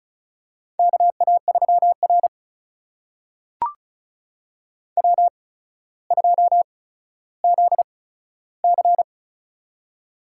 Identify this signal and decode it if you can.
Morse code, heard as a keyed tone, about 36 wpm, spaced as for 15 wpm, 700 Hz. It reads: KA3R W2ZC
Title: USA Call Signs - 4 Characters - 8x Character Spacing - 35wpm